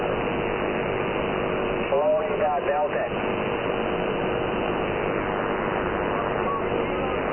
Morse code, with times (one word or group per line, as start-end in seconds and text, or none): none